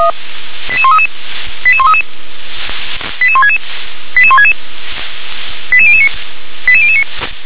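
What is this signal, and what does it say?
Five-tone selective call (ZVEI), heard as quick runs of five tones, tones of about 80 ms. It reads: 88133 88133 88166 88166 88009 88009
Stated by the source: ZVEI five-tone selective call 88911 received by SDR.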